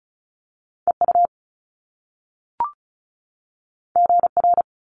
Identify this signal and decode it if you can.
Morse code, heard as a keyed tone, about 36 wpm, 700 Hz. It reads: EU GR
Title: Top Two Letter Combinations in Words 35wpm